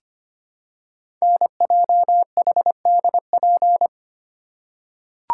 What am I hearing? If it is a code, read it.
Morse code, heard as a keyed tone, about 25 wpm, 700 Hz. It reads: NJHDP
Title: Code Groups - 5 Letters - 25wpm